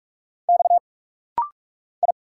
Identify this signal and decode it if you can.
Morse code, heard as a keyed tone, about 44 wpm, 690 Hz. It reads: X I
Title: Single Letter-Number - Rapid-Fire 45wpm